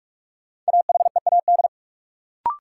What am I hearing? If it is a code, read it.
Morse code, heard as a keyed tone, about 47 wpm, 700 Hz. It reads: AHEAD
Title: Sentences from Top 1100 Words - New Word Drill Alt - 45wpm